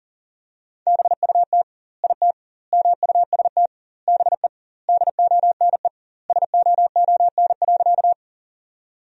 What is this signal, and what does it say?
Morse code, heard as a keyed tone, about 40 wpm, 700 Hz. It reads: BUT IT MUST BE DONE SOON.
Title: 2x Word-Spacing - Sentences from Top 500 Words - 40wpm